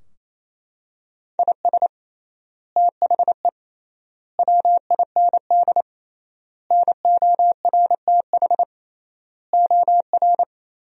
Morse code, heard as a keyed tone, about 28 wpm, 700 Hz. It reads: IS THE WIND NORTH OR
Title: 3x Word-Spacing - Sentences from Top 400 Words - 28wpm